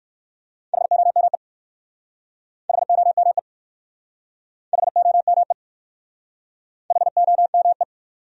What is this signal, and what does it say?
Morse code, heard as a keyed tone, about 61 wpm, 700 Hz. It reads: HOME HOME HOME HOME